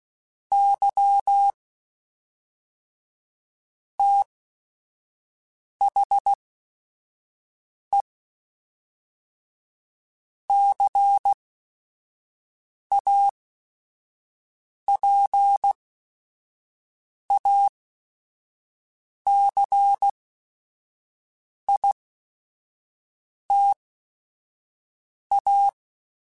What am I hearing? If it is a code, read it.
Morse code, heard as a keyed tone, about 16 wpm, spaced as for 5 wpm, 800 Hz. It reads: Y THE CAPACITA
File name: Technical Subjects / Technical Subjects-mp3s / XC @5